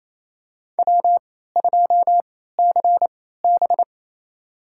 Morse code, heard as a keyed tone, about 27 wpm, spaced as for 16 wpm, 700 Hz. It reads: W2CB
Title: USA Call Signs - 3x Character Spacing - 28wpm